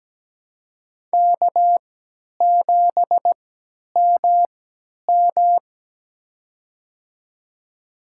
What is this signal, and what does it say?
Morse code, heard as a keyed tone, about 17 wpm, spaced as for 10 wpm, 700 Hz. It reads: K7MM
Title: USA Call Signs - 4 Characters - 3x Character Spacing - 17wpm